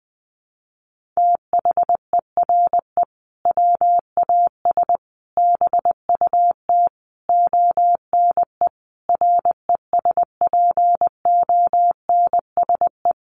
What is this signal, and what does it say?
Morse code, heard as a keyed tone, about 20 wpm, 700 Hz. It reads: THERE WAS BUT ONE RESPONSE
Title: Sentences from Top 1400 Words 20wpm